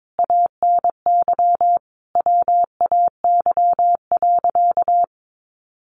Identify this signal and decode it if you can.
Morse code, heard as a keyed tone, about 22 wpm, 700 Hz. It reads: ANY WAY.
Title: Sentences from Top 1100 Words 22wpm